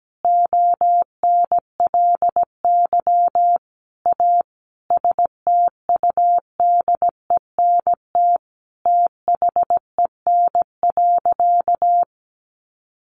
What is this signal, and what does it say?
Morse code, heard as a keyed tone, about 17 wpm, 700 Hz. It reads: ONLY A STUDENT THEN.
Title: Sentences from Top 1000 Words 17wpm